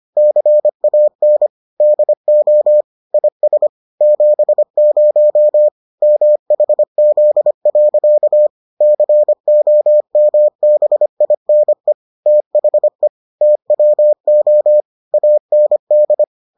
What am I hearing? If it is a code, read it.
Morse code, heard as a keyed tone, about 25 wpm, 600 Hz. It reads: CAN DO IS 70 MHZ. COMBINE THE TWO AND